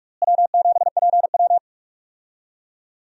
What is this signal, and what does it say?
Morse code, heard as a keyed tone, about 42 wpm, 700 Hz. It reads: W7PW
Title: USA Call Signs - 4 Characters - Repeated 3x - 45wpm